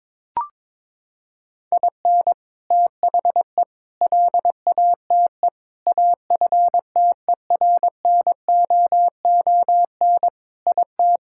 Morse code, heard as a keyed tone, about 22 wpm, 700 Hz. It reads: IN THE LATE AFTERNOON IT
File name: practice / 22wpm / Sentences with 3-Word Phrases - 22wpm